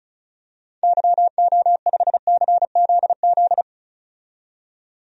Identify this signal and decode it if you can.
Morse code, heard as a keyed tone, about 36 wpm, 700 Hz. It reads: YO5CZZ
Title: DXCC-100 Call Signs - 6 Characters - 35wpm